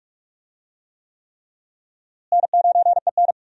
Morse code, heard as a keyed tone, about 46 wpm, 700 Hz. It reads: N9EN
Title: USA Call Signs - 4 Characters - Repeated 3x - 45wpm